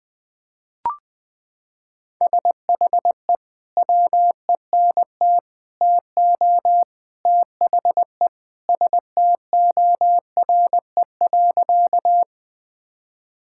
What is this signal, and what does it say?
Morse code, heard as a keyed tone, about 20 wpm, 700 Hz. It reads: SHE WENT TO THE STORE.
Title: Sentences with 3-Word Phrases - 20wpm